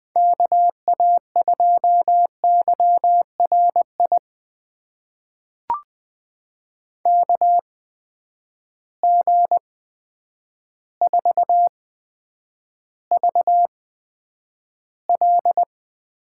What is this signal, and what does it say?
Morse code, heard as a keyed tone, about 20 wpm, spaced as for 9 wpm, 700 Hz. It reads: KA2YRI KG4VL
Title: USA Call Signs - 6 Characters - 8x Character Spacing - 20wpm